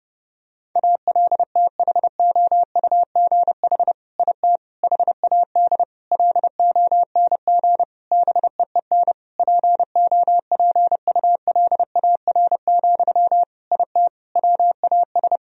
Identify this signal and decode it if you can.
Morse code, heard as a keyed tone, about 30 wpm, 700 Hz. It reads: ALTHOUGH IT HAD LONG BEEN POPULAR, IT WAS
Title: Sentences from Top 1000 Words - Review Part 3 - 30wpm